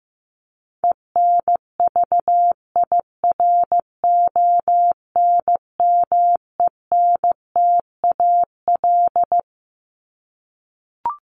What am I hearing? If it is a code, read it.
Morse code, heard as a keyed tone, about 15 wpm, 700 Hz. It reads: ENVIRONMENTAL
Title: Top 1001-1250 Words 15wpm